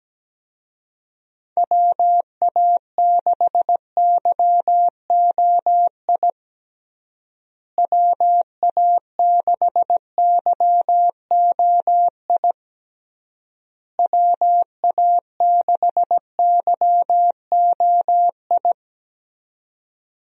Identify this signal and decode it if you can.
Morse code, heard as a keyed tone, about 17 wpm, 700 Hz. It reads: WA6YOI WA6YOI WA6YOI
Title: USA Call Signs - 6 Characters - Repeated 3x - 17wpm